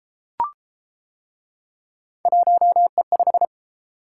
Morse code, heard as a keyed tone, about 31 wpm, 700 Hz. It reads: 1E5